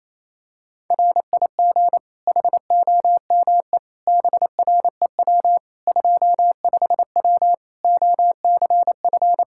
Morse code, heard as a keyed tone, about 28 wpm, 700 Hz. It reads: RIG HOME BREW 25W OCF
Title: Practice QSO - Exchange 2 - Info - 28wpm